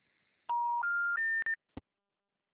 Special information tone.
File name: out- -2004- - - .98115